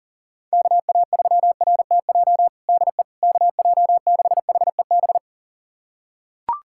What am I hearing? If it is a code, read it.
Morse code, heard as a keyed tone, about 40 wpm, 700 Hz. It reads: KA3RTJ DE KJ6HEB
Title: USA Callsign DE Callsign 40wpm